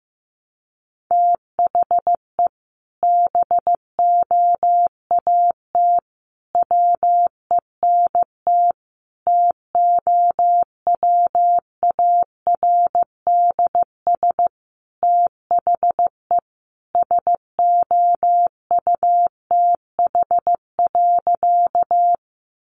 Morse code, heard as a keyed tone, about 15 wpm, 700 Hz. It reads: THE BOAT WENT TOWARDS THE SOUTH.
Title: Sentences from Top 1100 Words 15wpm